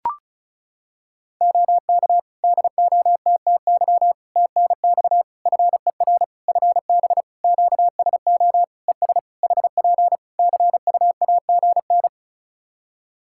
Morse code, heard as a keyed tone, about 36 wpm, 700 Hz. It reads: OK DOTTY TNX FER FB QSO ES HP CUAGN
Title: Practice QSO - Exchange 4 - Ending - 35wpm